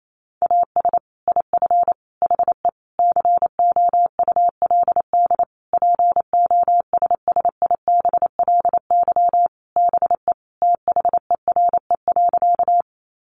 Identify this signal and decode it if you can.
Morse code, heard as a keyed tone, about 28 wpm, 700 Hz. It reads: AS IF HE COULD POSSIBLY BE THERE.
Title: Sentences from Top 1300 Words 28wpm